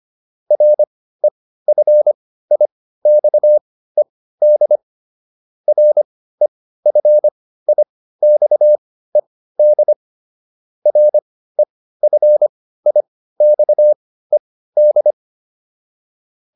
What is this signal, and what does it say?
Morse code, heard as a keyed tone, about 25 wpm, spaced as for 15 wpm, 600 Hz. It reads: REFIXED REFIXED REFIXED